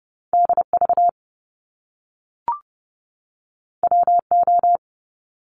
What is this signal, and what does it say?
Morse code, heard as a keyed tone, about 30 wpm, 700 Hz. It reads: DV WO
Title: Top Two Letter Combinations in Words 30wpm